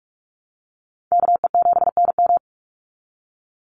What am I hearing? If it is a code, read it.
Morse code, heard as a keyed tone, about 46 wpm, 690 Hz. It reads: KE7NM
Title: USA Call Signs 45wpm